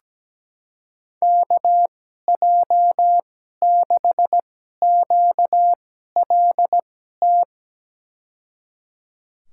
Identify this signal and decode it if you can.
Morse code, heard as a keyed tone, about 17 wpm, spaced as for 12 wpm, 700 Hz. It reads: KJ6QLT
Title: USA Call Signs - 6 Characters - 2x Character Spacing - 17wpm